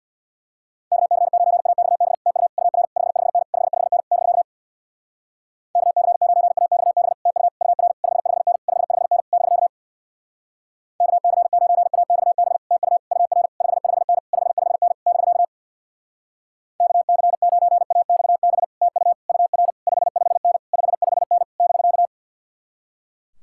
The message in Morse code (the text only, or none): KC9AXB TU UR 55N 55N <BK> KC9AXB TU UR 55N 55N <BK> KC9AXB TU UR 55N 55N <BK> KC9AXB TU UR 55N 55N <BK>